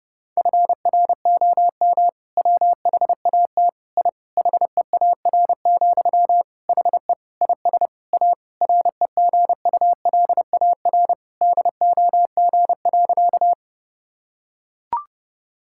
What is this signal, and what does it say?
Morse code, heard as a keyed tone, about 32 wpm, 700 Hz. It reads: FROM WHAT I HEAR, HE IS A REGULAR DOG.
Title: Sentences from Top 1000 Words - Review Part 6 - 30wpm